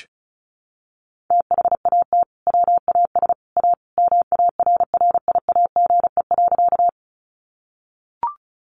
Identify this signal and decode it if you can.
Morse code, heard as a keyed tone, about 34 wpm, 700 Hz. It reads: THAT WAS A MARRIAGE.
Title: Sentences from Top 800 Words 35wpm